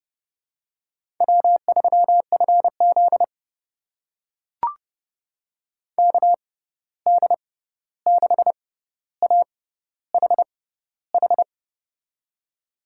Morse code, heard as a keyed tone, about 29 wpm, spaced as for 14 wpm, 700 Hz. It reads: W3FZ KD6AHH
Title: USA Call Signs - 6x Character Spacing - 30wpm